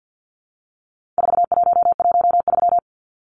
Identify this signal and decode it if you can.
Morse code, heard as a keyed tone, about 50 wpm, 700 Hz. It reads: 4113
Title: Numbers - 4-Digit 50wpm